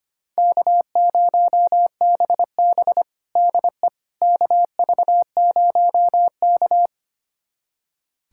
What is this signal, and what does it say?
Morse code, heard as a keyed tone, about 25 wpm, 700 Hz. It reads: K0BB DE KV0K
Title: USA Callsign DE Callsign - 4 Characters - 25wpm